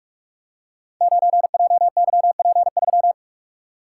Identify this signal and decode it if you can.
Morse code, heard as a keyed tone, about 44 wpm, 700 Hz. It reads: 9JYP3
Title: Code Groups - 5 Letter:Numbers - 45wpm